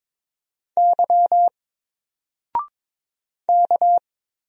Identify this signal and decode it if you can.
Morse code, heard as a keyed tone, about 22 wpm, 700 Hz. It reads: Y K
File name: Single-Letter-Number-22wpm